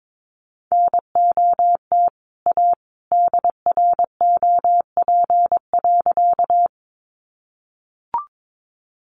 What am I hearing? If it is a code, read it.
Morse code, heard as a keyed tone, about 22 wpm, 700 Hz. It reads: NOT A DROP.
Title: Sentences from Top 1400 Words 22wpm